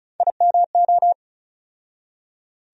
Morse code, heard as a keyed tone, about 36 wpm, 700 Hz. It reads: IMO
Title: USA Call Sign Postfixes - 3 Letters 35wpm